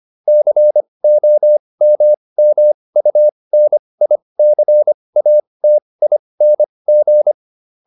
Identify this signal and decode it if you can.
Morse code, heard as a keyed tone, about 25 wpm, spaced as for 20 wpm, 600 Hz. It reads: COMMUNICATING